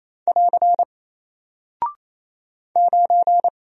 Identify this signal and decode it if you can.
Morse code, heard as a keyed tone, about 27 wpm, 700 Hz. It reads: <AR> 9
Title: Single Character (Letter:Number:Punctuation:Prosign) Alt 28wpm